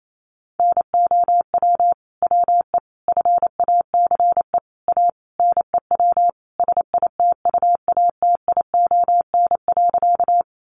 Morse code, heard as a keyed tone, about 28 wpm, 700 Hz. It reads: NOW WE FACE A NEW SITUATION.
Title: Sentences from Top 1200 Words 28wpm